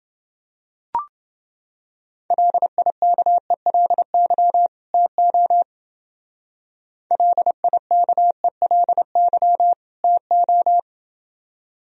Morse code, heard as a keyed tone, about 28 wpm, 700 Hz. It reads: LIKELY TO LIKELY TO